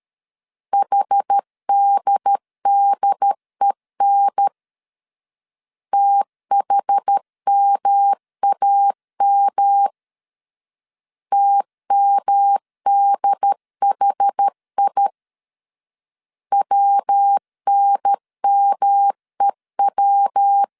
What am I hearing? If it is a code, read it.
Morse code, heard as a keyed tone, about 13 wpm, 800 Hz. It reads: HDDEN THMAM TMDHI WNMEW
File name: CW / DET / 1 3 1500C S C 12wpm 2750